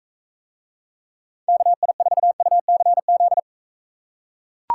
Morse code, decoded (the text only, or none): KI4UCZ